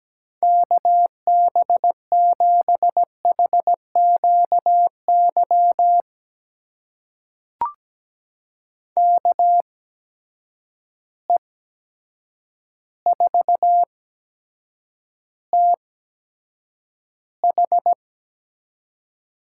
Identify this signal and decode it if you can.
Morse code, heard as a keyed tone, about 17 wpm, spaced as for 8 wpm, 700 Hz. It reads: KB7HQY KE4TH
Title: USA Call Signs - 6 Characters - 8x Character Spacing - 17wpm